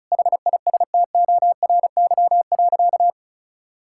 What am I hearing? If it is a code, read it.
Morse code, heard as a keyed tone, about 34 wpm, 700 Hz. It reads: HISTORY.
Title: Sentences from Top 1000 Words - Review Part 5 - 35wpm